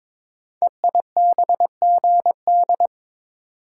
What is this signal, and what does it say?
Morse code, heard as a keyed tone, about 22 wpm, 700 Hz. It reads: EIBGD
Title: Code Groups - 5 Letters - 22wpm